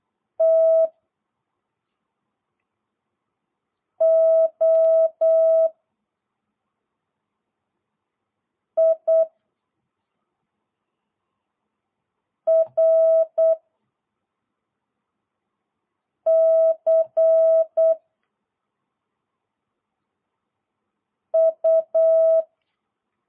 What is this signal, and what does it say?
Morse code, heard as a keyed tone, about 8 wpm, 650 Hz. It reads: T O I R C U